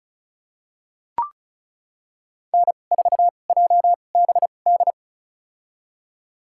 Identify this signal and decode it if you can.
Morse code, heard as a keyed tone, about 36 wpm, spaced as for 25 wpm, 700 Hz. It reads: N4JBD
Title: USA Call Signs - 5 Characters - 2x Character Spacing - 35wpm